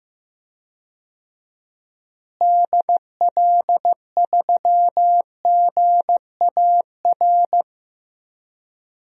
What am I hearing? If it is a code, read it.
Morse code, heard as a keyed tone, about 15 wpm, 700 Hz. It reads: DL3GAR
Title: DXCC-100 Call Signs - 6 Characters - Repeated 3x - 15wpm